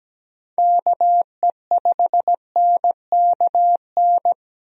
Morse code, heard as a keyed tone, about 17 wpm, 700 Hz. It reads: KE5NKN
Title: USA Call Signs - 6 Characters - Repeated 3x - 17wpm